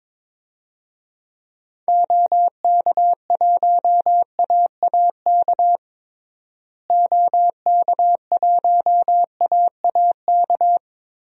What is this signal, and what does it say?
Morse code, heard as a keyed tone, about 22 wpm, 700 Hz. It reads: OK1AAK OK1AAK
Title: DXCC-100 Call Signs - 6 Characters - Repeated 2x - 22wpm